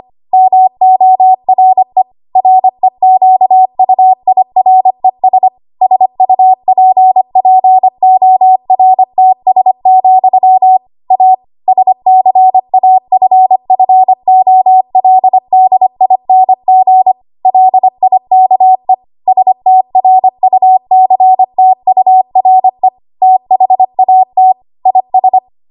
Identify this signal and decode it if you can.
Morse code, heard as a keyed tone, about 25 wpm, 750 Hz. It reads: MORE REQUIRES SUPPORTS, A SCAFFOLDING LIKE STRUCTURE THAT IS
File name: Archive / 25 WPM / 25WPM